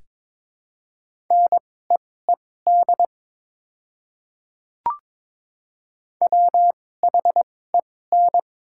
Morse code, heard as a keyed tone, about 22 wpm, spaced as for 16 wpm, 700 Hz. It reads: NEED WHEN
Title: 2x Character Spacing - Top 300 Words - 22wpm